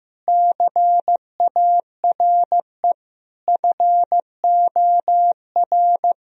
Morse code, heard as a keyed tone, about 15 wpm, 700 Hz. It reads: CARE FOR
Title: Sentences from Top 1000 Words - Review Part 3 - 15wpm